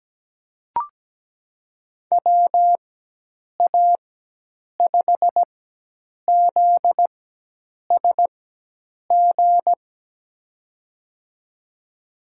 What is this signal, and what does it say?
Morse code, heard as a keyed tone, about 17 wpm, spaced as for 8 wpm, 700 Hz. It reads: WA5ZSG
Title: USA Call Signs - 6 Characters - 4x Character Spacing - 17wpm